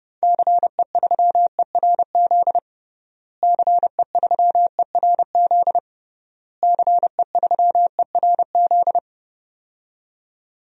Morse code, heard as a keyed tone, about 30 wpm, 700 Hz. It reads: CE3ERZ CE3ERZ CE3ERZ
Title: DXCC-100 Call Signs - 6 Characters - Repeated 3x - 30wpm